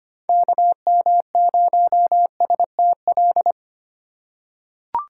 Morse code, heard as a keyed tone, about 25 wpm, 700 Hz. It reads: KM0STL
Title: USA Call Signs - 6 Characters - 4x Character Spacing - 25wpm